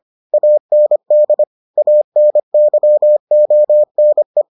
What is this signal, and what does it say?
Morse code, heard as a keyed tone, about 25 wpm, 600 Hz. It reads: AND ANYONE